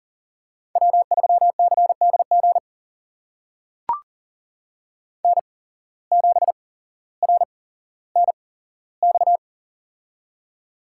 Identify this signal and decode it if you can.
Morse code, heard as a keyed tone, about 40 wpm, spaced as for 17 wpm, 700 Hz. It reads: W3CDG N7RNX
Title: USA Call Signs - 8x Character Spacing - 40wpm